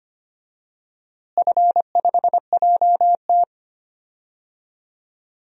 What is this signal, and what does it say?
Morse code, heard as a keyed tone, about 25 wpm, 700 Hz. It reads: F5JT
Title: DXCC-100 Call Signs - 4 Characters - Repeated 3x - 25wpm